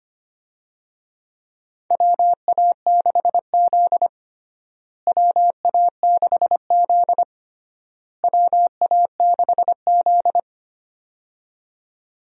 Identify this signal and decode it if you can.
Morse code, heard as a keyed tone, about 25 wpm, 700 Hz. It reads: WA6Z WA6Z WA6Z
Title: USA Call Signs - 4 Characters - Repeated 3x - 25wpm